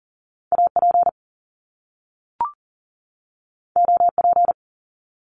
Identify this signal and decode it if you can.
Morse code, heard as a keyed tone, about 39 wpm, 700 Hz. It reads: AP OP
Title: Top Two Letter Combinations in Words 40wpm